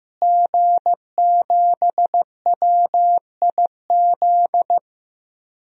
Morse code, heard as a keyed tone, about 16 wpm, 700 Hz. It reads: G7WIZ
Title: DXCC-100 Call Signs - 5 Characters - Repeated 3x - 15wpm